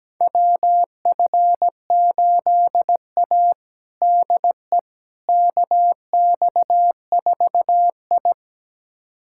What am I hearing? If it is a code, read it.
Morse code, heard as a keyed tone, about 17 wpm, 700 Hz. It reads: WF8A DE KX4I
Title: USA Callsign DE Callsign - 4 Characters - 17wpm